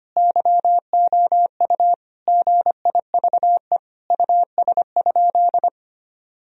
Morse code, heard as a keyed tone, about 25 wpm, 700 Hz. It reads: YOU GIVE US?
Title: Sentences from Top 1000 Words - Review Part 4 - 25wpm